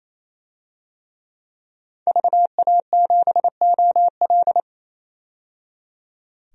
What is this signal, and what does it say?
Morse code, heard as a keyed tone, about 28 wpm, 700 Hz. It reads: VA7OL
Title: DXCC-100 Call Signs - 5 Characters - 28wpm